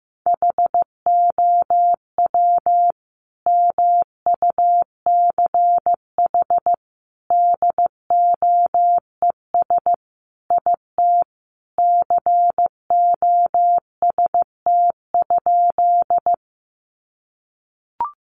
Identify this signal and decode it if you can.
Morse code, heard as a keyed tone, about 15 wpm, 700 Hz. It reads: HOW MUCH DOES IT COST?
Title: Sentences from Top 800 Words 15wpm